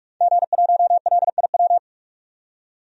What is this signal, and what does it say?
Morse code, heard as a keyed tone, about 47 wpm, 700 Hz. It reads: G1LIW
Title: DXCC-100 Call Signs - 5 Characters - 45wpm